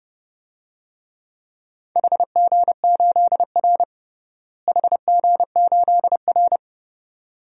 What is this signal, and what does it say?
Morse code, heard as a keyed tone, about 30 wpm, 700 Hz. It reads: HG8R HG8R
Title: DXCC-100 Call Signs - 4 Characters - Repeated 2x - 30wpm